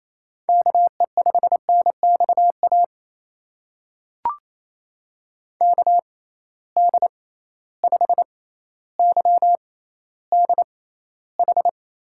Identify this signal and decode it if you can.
Morse code, heard as a keyed tone, about 28 wpm, spaced as for 14 wpm, 700 Hz. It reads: KE5NXA KD5YDH
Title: USA Call Signs - 6 Characters - 6x Character Spacing - 28wpm